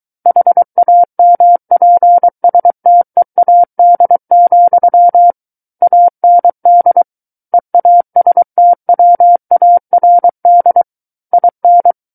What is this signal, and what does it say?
Morse code, heard as a keyed tone, about 23 wpm, 700 Hz. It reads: HAMPSTEAD, AND EASTWARD IN
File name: Book One / 14-Chapter 14 - 23 WPM